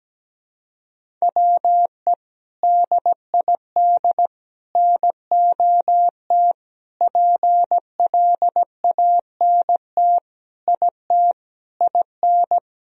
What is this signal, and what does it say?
Morse code, heard as a keyed tone, about 17 wpm, 700 Hz. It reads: WE DID NOT PLANT IT IN
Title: Sentences from Top 1000 Words - Review Part 1 - 17wpm